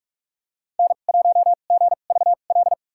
Morse code, heard as a keyed tone, about 46 wpm, spaced as for 32 wpm, 700 Hz. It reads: N1GVL
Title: USA Call Signs - 5 Characters - 2x Character Spacing - 45wpm